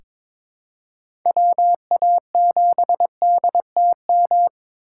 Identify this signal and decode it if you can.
Morse code, heard as a keyed tone, about 21 wpm, 700 Hz. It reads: WA7DTM
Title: USA Call Signs - 6 Characters - 6x Character Spacing - 22wpm